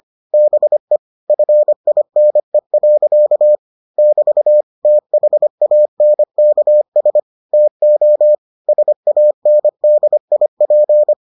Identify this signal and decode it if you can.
Morse code, heard as a keyed tone, about 25 wpm, 600 Hz. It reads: BE FINE. <BT> THANKS TO SANDIP